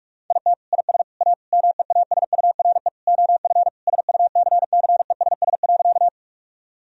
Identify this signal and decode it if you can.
Morse code, heard as a keyed tone, about 46 wpm, 700 Hz. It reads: IT IS A MEASURE OF SUCCESS.
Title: Sentences from Top 1000 Words - Review Part 2 - 45wpm